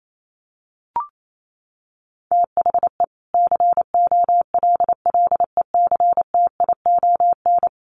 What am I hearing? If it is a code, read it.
Morse code, heard as a keyed tone, about 28 wpm, 700 Hz. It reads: THE COLLECTION
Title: Sentences from Top 1200 Words 28wpm